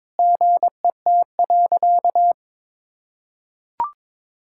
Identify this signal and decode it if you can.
Morse code, heard as a keyed tone, about 21 wpm, 700 Hz. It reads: GET.
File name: practice / 22wpm / Sentences with 8-Word Phrases - 22wpm